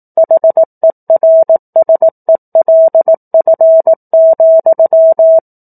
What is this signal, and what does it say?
Morse code, heard as a keyed tone, about 18 wpm, 650 Hz. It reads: HERSELF,
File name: 04-Chapter-04 18 WPM